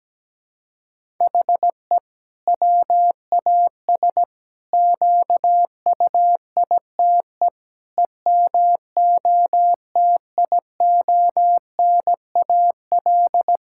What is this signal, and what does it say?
Morse code, heard as a keyed tone, about 17 wpm, 700 Hz. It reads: HE WAS QUITE EMOTIONAL
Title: Sentences from Top 1000 Words - Review Part 5 - 17wpm